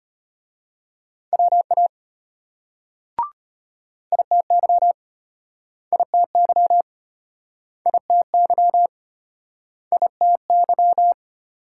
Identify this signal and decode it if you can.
Morse code, heard as a keyed tone, about 34 wpm, 700 Hz. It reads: WA ITY ITY ITY ITY